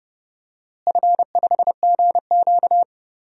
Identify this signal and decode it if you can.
Morse code, heard as a keyed tone, about 30 wpm, 700 Hz. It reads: F5GQ
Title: DXCC-100 Call Signs - 4 Characters - Repeated 3x - 30wpm